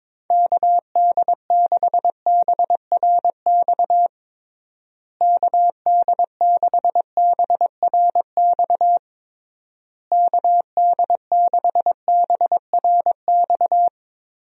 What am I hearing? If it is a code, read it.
Morse code, heard as a keyed tone, about 22 wpm, 700 Hz. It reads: KD6BRX KD6BRX KD6BRX
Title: USA Call Signs - 6 Characters - Repeated 3x - 22wpm